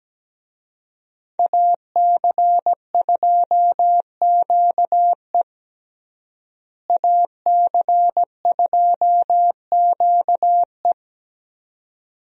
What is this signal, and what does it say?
Morse code, heard as a keyed tone, about 17 wpm, 700 Hz. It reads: AC2QE AC2QE
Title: USA Call Signs - 5 Characters - Repeated 2x - 17wpm